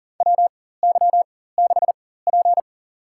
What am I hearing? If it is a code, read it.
Morse code, heard as a keyed tone, about 39 wpm, spaced as for 18 wpm, 700 Hz. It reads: WY6P
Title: USA Call Signs - 4 Characters - 4x Character Spacing - 40wpm